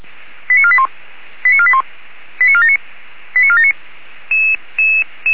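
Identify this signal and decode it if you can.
Five-tone selective call (ZVEI), heard as quick runs of five tones, tones of about 70 ms. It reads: 89481 89481 89479 89479
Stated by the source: Received on an SDR, from the Feuerwehr Orlamünde audio archive.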